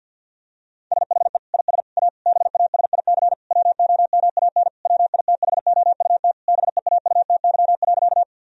Morse code, heard as a keyed tone, about 50 wpm, 700 Hz. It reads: SHE IS A BASIC WOMAN WITHOUT BEAUTY.
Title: Sentences from Top 1000 Words - Review Part 3 - 50wpm